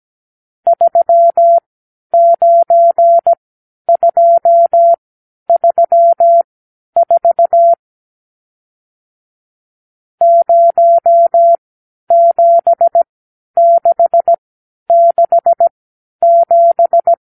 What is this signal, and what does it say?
Morse code, heard as a keyed tone, about 17 wpm, spaced as for 10 wpm, 680 Hz. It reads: 39234 07667
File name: arrl / Review (1-0)